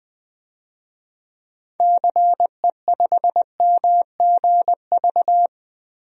Morse code, heard as a keyed tone, about 20 wpm, 700 Hz. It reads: CE5MGV
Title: DXCC-100 Call Signs - 6 Characters - Repeated 3x - 20wpm